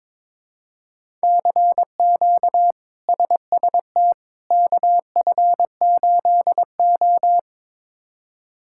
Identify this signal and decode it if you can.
Morse code, heard as a keyed tone, about 22 wpm, 700 Hz. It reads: CQ SST KF8O
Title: K1USN SST QSO Practice - 22wpm